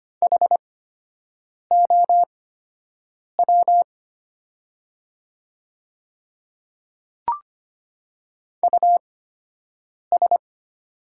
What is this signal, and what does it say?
Morse code, heard as a keyed tone, about 25 wpm, spaced as for 7 wpm, 690 Hz. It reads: HOW US
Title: 8x Character Spacing - Top 200 Words - 25wpm